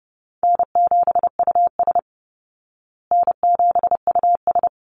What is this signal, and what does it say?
Morse code, heard as a keyed tone, about 30 wpm, 700 Hz. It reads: N7US N7US
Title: Call Signs Repeated 2x - 30wpm